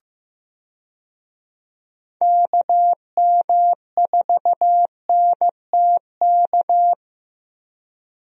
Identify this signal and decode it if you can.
Morse code, heard as a keyed tone, about 15 wpm, 700 Hz. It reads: KM4NTK
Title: USA Call Signs - 6 Characters - Repeated 2x - 15wpm